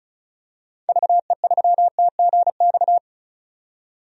Morse code, heard as a keyed tone, about 36 wpm, 700 Hz. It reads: VE3TGX
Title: DXCC-100 Call Signs - 6 Characters - 35wpm